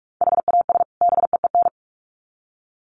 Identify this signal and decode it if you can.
Morse code, heard as a keyed tone, about 44 wpm, 700 Hz. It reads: HAS BEEN
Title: Sets of 2 Words 45wpm